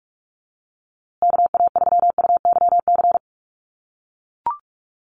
Morse code, heard as a keyed tone, about 44 wpm, 700 Hz. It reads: KA3UYC
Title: USA Call Signs Repeated 2x 45wpm